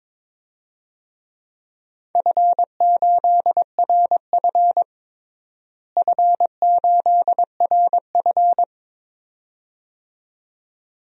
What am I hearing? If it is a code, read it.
Morse code, heard as a keyed tone, about 23 wpm, 700 Hz. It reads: F8RF F8RF
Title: DXCC-100 Call Signs - 4 Characters - Repeated 2x - 22wpm